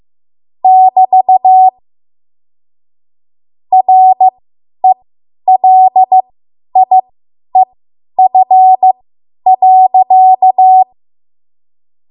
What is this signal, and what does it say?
Morse code, heard as a keyed tone, about 15 wpm, spaced as for 10 wpm, 750 Hz. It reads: <BT> RELIEF.
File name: Archive / 10 WPM / 10WPM